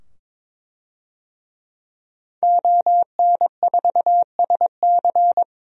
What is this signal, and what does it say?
Morse code, heard as a keyed tone, about 22 wpm, 700 Hz. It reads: ON4SC
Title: DXCC-100 Call Signs - 5 Characters - 22wpm